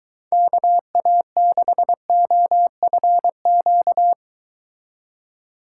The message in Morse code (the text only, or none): KA6OFQ